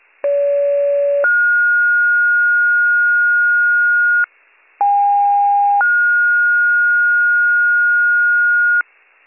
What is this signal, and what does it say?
Two-tone page, heard as a one-second tone, then a three-second tone, about 570 then 1430 Hz, about 810 then 1430 Hz, with noise. It is heard from 0.2 to 8.8 s.